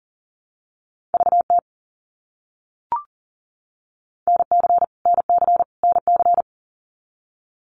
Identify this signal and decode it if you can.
Morse code, heard as a keyed tone, about 40 wpm, 700 Hz. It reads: VT NC NC NC
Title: US States Repeated 3x 40wpm